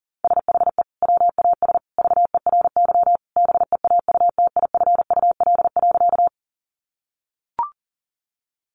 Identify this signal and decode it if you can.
Morse code, heard as a keyed tone, about 40 wpm, 700 Hz. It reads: SHE WAS VERY BEAUTIFUL.
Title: Sentences from Top 900 Words 40wpm